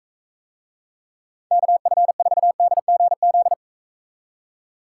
Morse code, heard as a keyed tone, about 39 wpm, 700 Hz. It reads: KF4DGZ